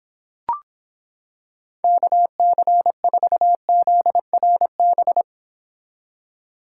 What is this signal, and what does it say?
Morse code, heard as a keyed tone, about 26 wpm, 700 Hz. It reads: KC4ZRB